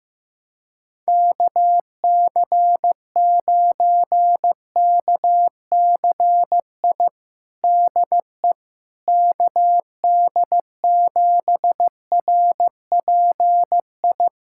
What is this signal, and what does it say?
Morse code, heard as a keyed tone, about 15 wpm, 700 Hz. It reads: KC9KCI DE KD7RPI
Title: USA Callsign DE Callsign 15wpm